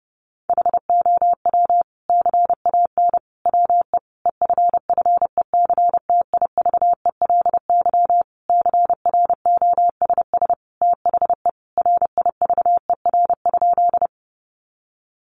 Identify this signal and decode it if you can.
Morse code, heard as a keyed tone, about 30 wpm, 700 Hz. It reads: HOW CAN WE EFFECTIVELY CROSS THE RIVER?
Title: Sentences from Top 1400 Words 30wpm